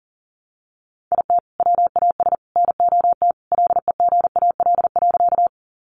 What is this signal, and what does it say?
Morse code, heard as a keyed tone, about 40 wpm, 700 Hz. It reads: IT WAS NOT LEGAL.
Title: Sentences from Top 1200 Words 40wpm